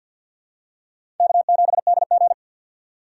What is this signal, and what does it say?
Morse code, heard as a keyed tone, about 50 wpm, 700 Hz. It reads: K7DG
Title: USA Call Signs - 4 Characters - 2x Character Spacing - 50wpm